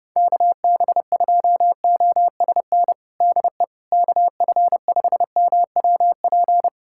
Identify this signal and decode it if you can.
Morse code, heard as a keyed tone, about 30 wpm, 700 Hz. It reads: KB2OSN DE KF5MWP
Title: USA Callsign DE Callsign 30wpm